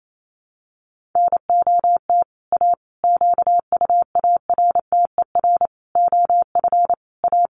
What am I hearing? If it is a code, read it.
Morse code, heard as a keyed tone, about 28 wpm, 700 Hz. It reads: NOT A QUARTER OF A
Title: Sentences from Top 1300 Words 28wpm